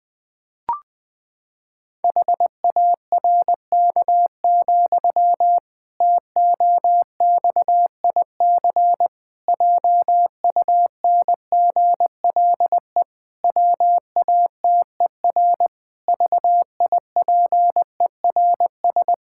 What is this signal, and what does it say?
Morse code, heard as a keyed tone, about 20 wpm, 700 Hz. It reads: HARK, TOXIC JUNGLE WATER VIPERS
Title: Pangram Challenge v2 - 20wpm